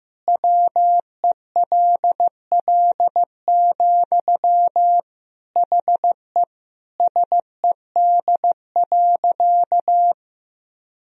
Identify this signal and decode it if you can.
Morse code, heard as a keyed tone, about 15 wpm, 700 Hz. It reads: WELL, HE SED.
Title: Sentences from Top 200 Words and Abbreviations 15wpm